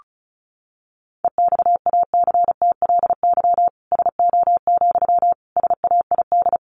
Morse code, heard as a keyed tone, about 34 wpm, 700 Hz. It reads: EXACTLY SO, SAID
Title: Sentences from Top 900 Words 35wpm